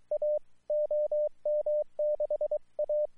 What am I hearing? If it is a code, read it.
Morse code, heard as a keyed tone, about 23 wpm, 600 Hz, with radio noise. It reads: A OM6A
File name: iw0ffk / 2012 09 02 OM6A IW0FFK JN99 SIDE